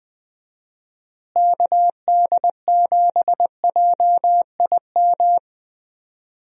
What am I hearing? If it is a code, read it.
Morse code, heard as a keyed tone, about 20 wpm, 700 Hz. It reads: KD7JIM
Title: USA Call Signs - 6 Characters - Repeated 3x - 20wpm